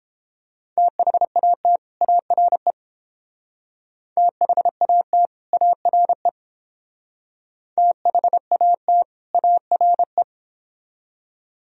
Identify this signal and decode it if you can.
Morse code, heard as a keyed tone, about 28 wpm, 700 Hz. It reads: THAT ARE THAT ARE THAT ARE